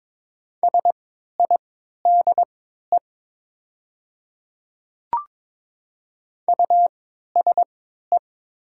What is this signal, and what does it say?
Morse code, heard as a keyed tone, about 22 wpm, spaced as for 13 wpm, 700 Hz. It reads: SIDE USE
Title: 3x Character Spacing - Top 100 Words - 22wpm